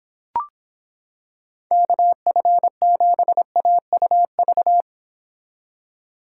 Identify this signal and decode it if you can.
Morse code, heard as a keyed tone, about 26 wpm, 700 Hz. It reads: KF7AUV